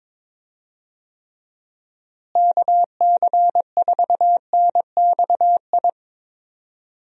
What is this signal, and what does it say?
Morse code, heard as a keyed tone, about 21 wpm, 700 Hz. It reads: KC4NXI
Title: USA Call Signs - 6 Characters - Repeated 2x - 22wpm